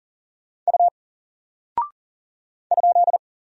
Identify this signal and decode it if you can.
Morse code, heard as a keyed tone, about 40 wpm, 700 Hz. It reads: U ?
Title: Single Character (Letter:Number:Punctuation:Prosign) Alt 40wpm